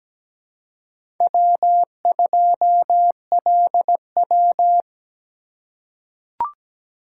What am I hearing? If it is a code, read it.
Morse code, heard as a keyed tone, about 17 wpm, 700 Hz. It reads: W2LW
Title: USA Call Signs - 3x Character Spacing - 17wpm